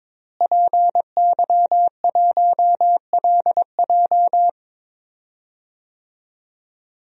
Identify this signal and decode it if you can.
Morse code, heard as a keyed tone, about 22 wpm, 700 Hz. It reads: PY1LJ
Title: DXCC-100 Call Signs - 5 Characters - Repeated 2x - 22wpm